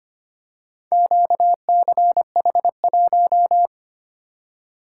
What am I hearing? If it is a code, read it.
Morse code, heard as a keyed tone, about 25 wpm, 700 Hz. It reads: QCH1
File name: practice / 25wpm / Four Letter-Number Combinations - 25wpm